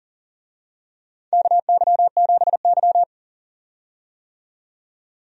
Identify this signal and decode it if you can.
Morse code, heard as a keyed tone, about 39 wpm, 700 Hz. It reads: KY7Y
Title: USA Call Signs - 4 Characters - 40wpm